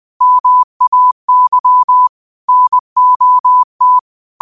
Morse code, heard as a keyed tone, about 20 wpm, 1000 Hz. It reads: MAY NOT
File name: qotdfiles / 20wpm